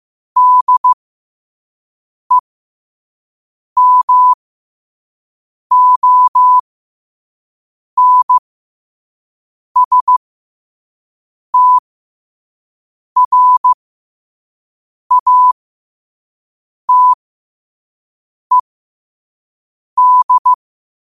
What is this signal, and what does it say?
Morse code, heard as a keyed tone, about 15 wpm, spaced as for 5 wpm, 1000 Hz. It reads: DEMONSTRATED
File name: qotdfiles / 5wpm